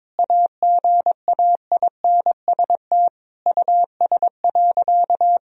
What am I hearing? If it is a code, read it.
Morse code, heard as a keyed tone, about 22 wpm, 700 Hz. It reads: AGAINST US.
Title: Sentences from Top 1000 Words - Review Part 2 - 22wpm